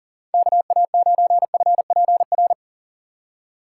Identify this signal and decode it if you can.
Morse code, heard as a keyed tone, about 39 wpm, 700 Hz. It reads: KA9FPR
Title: USA Call Signs - 6 Characters - 3x Character Spacing - 40wpm